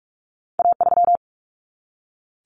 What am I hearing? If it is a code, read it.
Morse code, heard as a keyed tone, about 46 wpm, 700 Hz. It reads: A3
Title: Common Cut Numbers - 2-Digit 45wpm